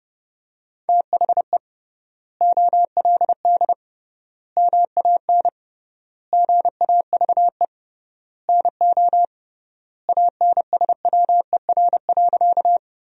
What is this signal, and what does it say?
Morse code, heard as a keyed tone, about 30 wpm, 700 Hz. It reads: THE OLD MAN GAVE NO ANSWER.
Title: 3x Word-Spacing - Sentences from Top 500 Words - 30wpm